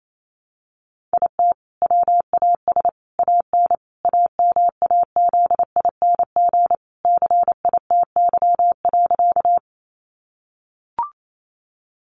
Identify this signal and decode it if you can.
Morse code, heard as a keyed tone, about 28 wpm, 700 Hz. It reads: IT WAS AN AMAZING CITY.
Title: Sentences from Top 1500 Words 28wpm